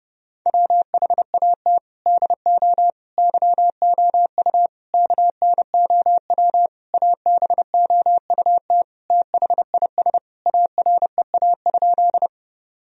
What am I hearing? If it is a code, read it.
Morse code, heard as a keyed tone, about 30 wpm, 700 Hz. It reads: WHAT DO YOU KNOW ABOUT THIS AREA?
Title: Sentences from Top 1000 Words - Review Part 2 - 30wpm